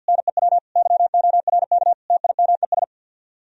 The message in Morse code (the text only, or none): NEW YORK TIMES